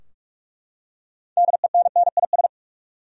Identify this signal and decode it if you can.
Morse code, heard as a keyed tone, about 42 wpm, 700 Hz. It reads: DENNIS
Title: USA Common Names - 45wpm